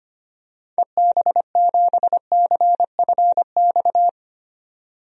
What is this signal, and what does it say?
Morse code, heard as a keyed tone, about 25 wpm, 700 Hz. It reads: EB7CFX
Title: DXCC-100 Call Signs - 6 Characters - 25wpm